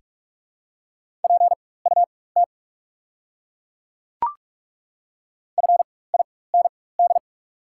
Morse code, heard as a keyed tone, about 42 wpm, spaced as for 21 wpm, 690 Hz. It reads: PUT FIND
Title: 4x Character Spacing - Top 300 Words - 45wpm